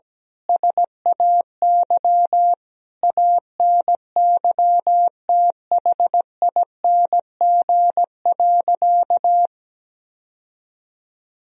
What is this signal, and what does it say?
Morse code, heard as a keyed tone, about 17 wpm, 700 Hz. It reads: SAY ANYTHING.
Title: Sentences from Top 1000 Words - Review Part 2 - 17wpm